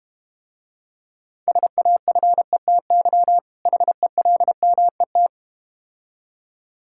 Morse code, heard as a keyed tone, about 31 wpm, 700 Hz. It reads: SAFETY HELMET
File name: Difficult Words - Speed-Racing with Hints - 25wpm